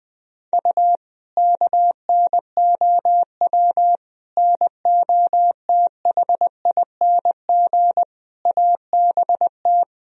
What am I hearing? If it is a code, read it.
Morse code, heard as a keyed tone, about 20 wpm, 700 Hz. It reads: U KNOW NOTHING ABT
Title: Sentences from Top 500 Words and Abbreviations 20wpm